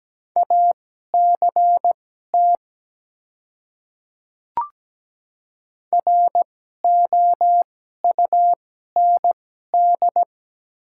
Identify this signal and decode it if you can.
Morse code, heard as a keyed tone, about 17 wpm, spaced as for 12 wpm, 700 Hz. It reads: ACT ROUND
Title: 2x Character Spacing - Top 300 Words - 17wpm